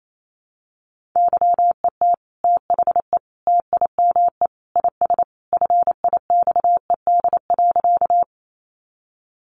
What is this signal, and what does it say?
Morse code, heard as a keyed tone, about 28 wpm, 700 Hz. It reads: YET THE TIME IS FIXED.
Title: Sentences from Top 1100 Words 28wpm